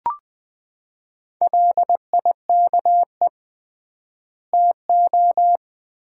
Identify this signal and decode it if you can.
Morse code, heard as a keyed tone, about 20 wpm, 700 Hz. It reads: LIKE TO
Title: 3x Word-Spacing - Sets of 2 Words - 20wpm